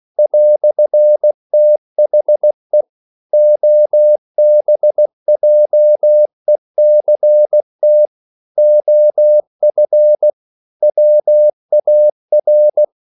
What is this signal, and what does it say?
Morse code, heard as a keyed tone, about 16 wpm, 600 Hz. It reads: "THE OBJECT OF WAR